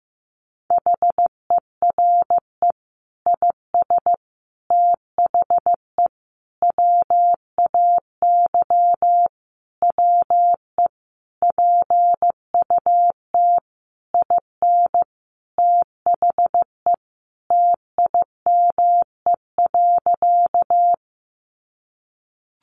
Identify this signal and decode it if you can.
Morse code, heard as a keyed tone, about 14 wpm, 700 Hz. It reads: HERE IS THE WAY WE PUT IN THE TIME.
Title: Sentences from Top 200 Words 15wpm